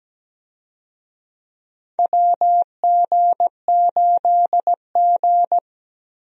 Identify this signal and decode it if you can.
Morse code, heard as a keyed tone, about 17 wpm, 700 Hz. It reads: WG8G
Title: USA Call Signs - 4 Characters - Repeated 2x - 17wpm